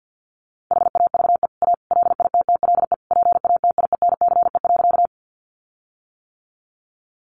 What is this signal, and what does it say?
Morse code, heard as a keyed tone, about 50 wpm, 700 Hz. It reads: HAVE A LITTLE PATIENCE.
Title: Sentences from Top 1300 Words 50wpm